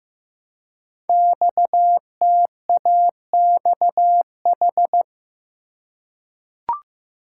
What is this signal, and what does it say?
Morse code, heard as a keyed tone, about 15 wpm, 700 Hz. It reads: XTAXH
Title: Code Groups - 5 Letters - 15wpm